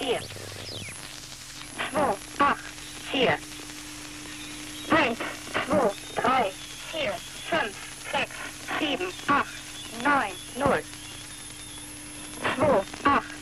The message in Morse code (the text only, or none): none